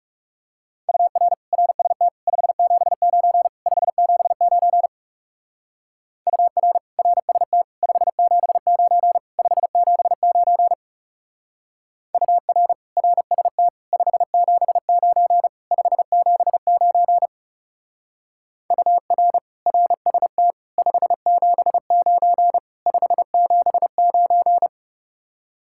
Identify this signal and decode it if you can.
Morse code, heard as a keyed tone, about 35 wpm, 700 Hz. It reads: UR RST 579 579 UR RST 579 579 UR RST 579 579 UR RST 579 579
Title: Speed-Racing Common QSO Phrases - 30wpm